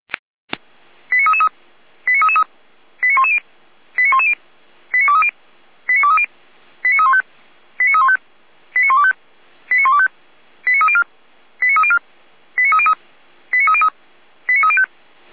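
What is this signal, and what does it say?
Five-tone selective call (ZVEI), heard as quick runs of five tones, tones of about 70 ms. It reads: 89333 89333 89110 89110 89230 89230 89315 89315 89125 89125 89304 89304 89303 89303 89305
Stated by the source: ZVEI five-tone selective call 89265 received on an SDR.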